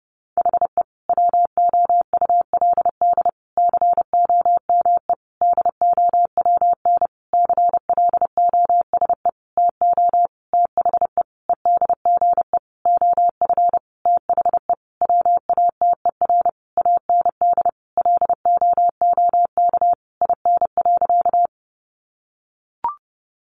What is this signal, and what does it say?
Morse code, heard as a keyed tone, about 28 wpm, 700 Hz. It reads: HE WOULD COME DOWN CLOSE TO THE EDGE OF THE WATER AND LOOK IN.
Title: Sentences from Top 1300 Words 30wpm